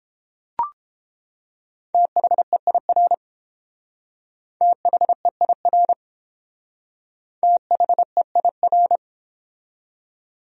Morse code, heard as a keyed tone, about 29 wpm, 700 Hz. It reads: THEIR THEIR THEIR